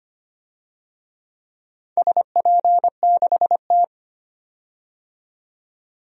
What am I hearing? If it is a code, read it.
Morse code, heard as a keyed tone, about 25 wpm, 700 Hz. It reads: SP6T
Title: DXCC-100 Call Signs - 4 Characters - 25wpm